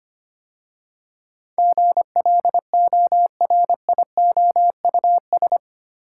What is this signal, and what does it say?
Morse code, heard as a keyed tone, about 25 wpm, 700 Hz. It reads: GLORIOUS
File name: 8 Letter Words - First 3 Letters Given - 25wpm